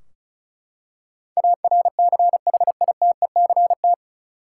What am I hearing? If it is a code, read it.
Morse code, heard as a keyed tone, about 34 wpm, 700 Hz. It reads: ARCHITECT
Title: Practice QSO - Exchange 3 - More Info - 35wpm